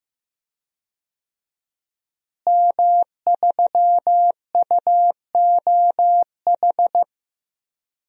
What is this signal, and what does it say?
Morse code, heard as a keyed tone, about 16 wpm, 700 Hz. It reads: M3UOH
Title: DXCC-100 Call Signs - 5 Characters - Repeated 2x - 15wpm